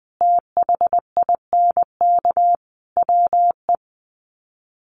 Morse code, heard as a keyed tone, about 20 wpm, 700 Hz. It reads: THINK WE
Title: Sets of 4 Words 20wpm